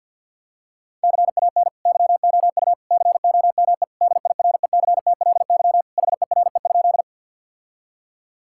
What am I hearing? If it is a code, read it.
Morse code, heard as a keyed tone, about 52 wpm, 700 Hz. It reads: CAN YOU COME DIRECTLY HERE?
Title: Sentences from Top 1000 Words - Review Part 5 - 50wpm